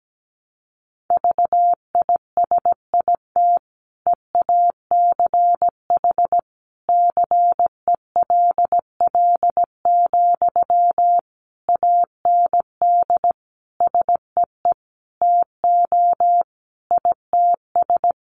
Morse code, heard as a keyed tone, about 17 wpm, 700 Hz. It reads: VISIT EACH CELL, AND SEE TO ITS
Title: Sentences from Top 1100 Words 17wpm